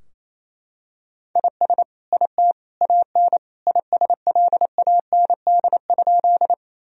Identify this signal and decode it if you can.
Morse code, heard as a keyed tone, about 27 wpm, 700 Hz. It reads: IS IT AN ISLAND?
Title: Sentences from Top 1000 Words - Review Part 3 - 28wpm